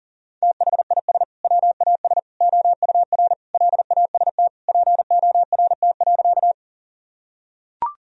Morse code, heard as a keyed tone, about 40 wpm, 700 Hz. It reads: THIS WAS OUR LAST PORT.
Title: Sentences from Top 1000 Words - Review Part 1 - 40wpm